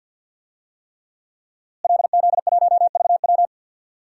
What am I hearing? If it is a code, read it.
Morse code, heard as a keyed tone, about 52 wpm, 700 Hz. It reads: LZ1VW
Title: DXCC-100 Call Signs - 5 Characters - Repeated 3x - 50wpm